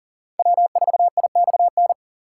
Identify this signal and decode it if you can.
Morse code, heard as a keyed tone, about 48 wpm, 700 Hz. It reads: W4IXN
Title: Speed Racing - USA 1x3 Call Signs 40wpm